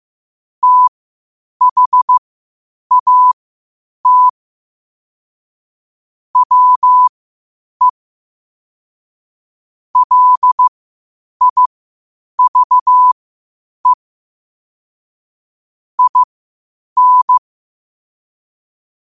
Morse code, heard as a keyed tone, about 15 wpm, spaced as for 8 wpm, 1000 Hz. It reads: THAT WE LIVE IN
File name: qotdfiles / 8wpm